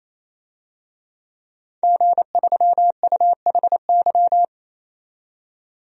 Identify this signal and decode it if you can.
Morse code, heard as a keyed tone, about 28 wpm, 700 Hz. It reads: G3UHY
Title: DXCC-100 Call Signs - 5 Characters - 28wpm